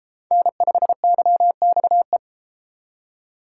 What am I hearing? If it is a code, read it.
Morse code, heard as a keyed tone, about 31 wpm, 700 Hz. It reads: N5YXE